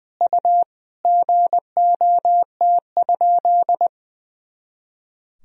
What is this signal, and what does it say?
Morse code, heard as a keyed tone, about 20 wpm, 700 Hz. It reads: U GOT?
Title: Sentences from Top 300 Words and Abbreviations 20wpm